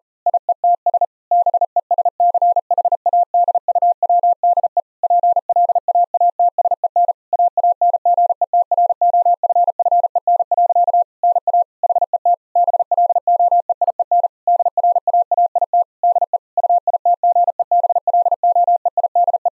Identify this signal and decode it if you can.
Morse code, heard as a keyed tone, about 40 wpm, 700 Hz. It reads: IETS BESCHADUWDE PLAATSEN AANGETROFFEN. NA HET BLOEIEN DRAAIT DE UITGEBLOEIDE
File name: Zendamateurnieuws en een stukje geseinde morse